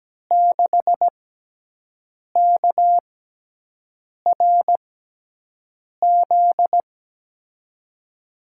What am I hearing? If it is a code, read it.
Morse code, heard as a keyed tone, about 17 wpm, spaced as for 6 wpm, 700 Hz. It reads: 6KRZ
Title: USA Call Signs - 6 Characters - 6x Character Spacing - 17wpm